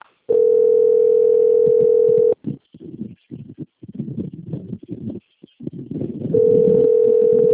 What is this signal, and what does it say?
Ringback tone (North America), 440 + 480 Hz.